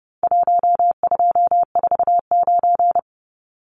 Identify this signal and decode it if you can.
Morse code, heard as a keyed tone, about 30 wpm, 700 Hz. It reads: 1249
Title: Numbers - 4-Digit 30wpm